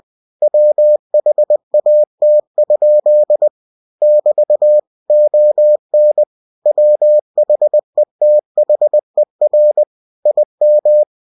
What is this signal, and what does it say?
Morse code, heard as a keyed tone, about 20 wpm, 600 Hz. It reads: WHAT? <BT> ON WHETHER IM
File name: cw-books / winnie-the-pooh / winnie-20wpm-page-0099